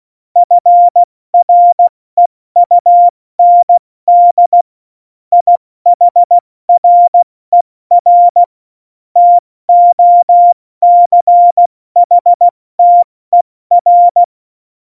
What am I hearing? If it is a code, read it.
Morse code, heard as a keyed tone, about 16 wpm, 700 Hz. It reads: FREUND IHRER TOCHTER
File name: Morsekurs / Witze / Witz 09. Juli 2019, 70 F80